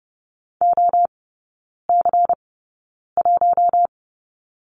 Morse code, heard as a keyed tone, about 30 wpm, spaced as for 9 wpm, 690 Hz. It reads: OC1
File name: Lesson 17e - Letter u - Three Characters 30wpm